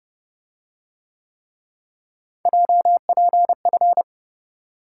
Morse code, heard as a keyed tone, about 29 wpm, 700 Hz. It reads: JPF